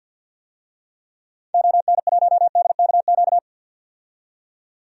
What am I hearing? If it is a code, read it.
Morse code, heard as a keyed tone, about 47 wpm, 700 Hz. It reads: ON1DKQ
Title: DXCC-100 Call Signs - 6 Characters - 50wpm